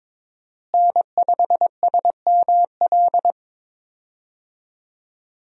Morse code, heard as a keyed tone, about 22 wpm, 700 Hz. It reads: N5SML
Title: USA Call Signs - 5 Characters - Repeated 2x - 22wpm